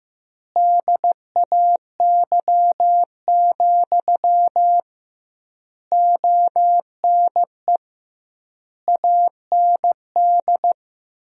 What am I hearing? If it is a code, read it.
Morse code, heard as a keyed tone, about 15 wpm, 700 Hz. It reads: DAY, ONE AND
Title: 2x Word-Spacing - Sentences from Top 200 Words - 15wpm